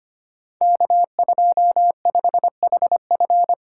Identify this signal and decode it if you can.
Morse code, heard as a keyed tone, about 25 wpm, 700 Hz. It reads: K25HF
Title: Code Groups - 5 Letter:Numbers - 25wpm